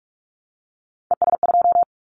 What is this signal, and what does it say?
Morse code, heard as a keyed tone, about 46 wpm, 700 Hz. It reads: ES2
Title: Callsign Prefixes for Top 50 DXCC - 45wpm